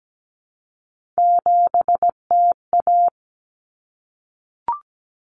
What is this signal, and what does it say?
Morse code, heard as a keyed tone, about 17 wpm, 690 Hz. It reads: 7TA
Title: Common Cut Numbers - 3-Digit 17wpm